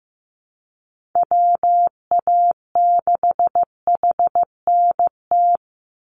Morse code, heard as a keyed tone, about 15 wpm, 700 Hz. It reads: WA6HNT
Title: USA Call Signs 15wpm